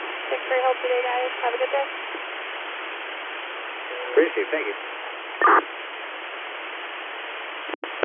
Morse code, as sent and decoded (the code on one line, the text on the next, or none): none